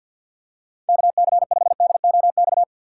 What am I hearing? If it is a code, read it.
Morse code, heard as a keyed tone, about 50 wpm, 700 Hz. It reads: KC5DOX
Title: USA Call Signs - 6 Characters - 3x Character Spacing - 50wpm